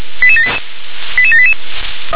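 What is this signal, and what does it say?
Five-tone selective call (ZVEI), heard as quick runs of five tones, tones of about 70 ms. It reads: 88788
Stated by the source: ZVEI five-tone selective call received by SDR.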